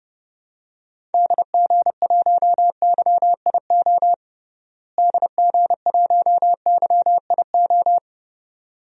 Morse code, heard as a keyed tone, about 32 wpm, 700 Hz. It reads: DG1YIO DG1YIO
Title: DXCC-100 Call Signs - 6 Characters - Repeated 2x - 30wpm